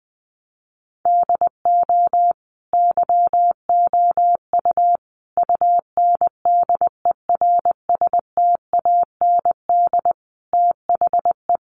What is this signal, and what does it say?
Morse code, heard as a keyed tone, about 20 wpm, 700 Hz. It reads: DO YOU UNDERSTAND THE
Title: Sentences from Top 1200 Words 20wpm